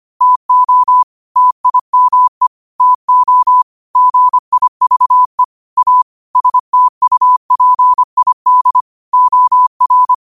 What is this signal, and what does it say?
Morse code, heard as a keyed tone, about 25 wpm, 1000 Hz. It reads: TO TIME TO GIVE A STUPID OR